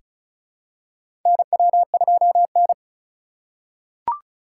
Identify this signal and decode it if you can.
Morse code, heard as a keyed tone, about 36 wpm, 700 Hz. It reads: NW2N
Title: USA Call Signs - 3x Character Spacing - 35wpm